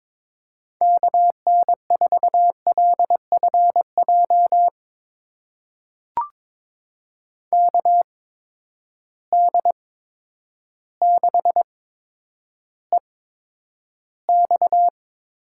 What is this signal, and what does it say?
Morse code, heard as a keyed tone, about 22 wpm, spaced as for 10 wpm, 700 Hz. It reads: KN4LFJ KD6EX
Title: USA Call Signs - 6 Characters - 8x Character Spacing - 22wpm